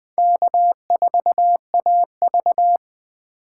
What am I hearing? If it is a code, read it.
Morse code, heard as a keyed tone, about 20 wpm, 700 Hz. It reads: K4AV
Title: USA Call Signs - 4 Characters - 3x Character Spacing - 20wpm